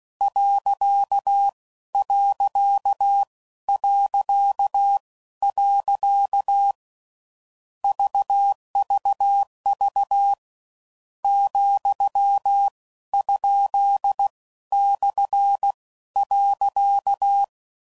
Morse code, heard as a keyed tone, about 16 wpm, spaced as for 12 wpm, 800 Hz. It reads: .... VVV ,?/.
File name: Mixed / Misc / Punct